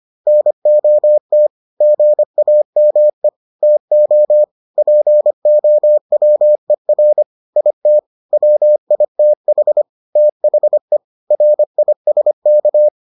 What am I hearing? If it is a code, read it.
Morse code, heard as a keyed tone, about 25 wpm, 600 Hz. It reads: NOT GAME TO POWER IT WITH THE RISK